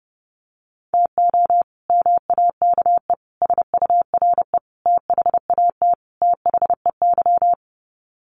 Morse code, heard as a keyed tone, about 30 wpm, 700 Hz. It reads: TO MAKE SURE THAT THEY
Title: Sets of 5 Words Encore2 30wpm